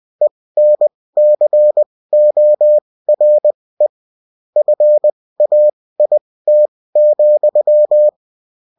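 Morse code, heard as a keyed tone, about 20 wpm, spaced as for 16 wpm, 600 Hz. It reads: ENCORE FAIT,
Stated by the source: Morse code bulletin of a French amateur CW union, sent as text and in CW.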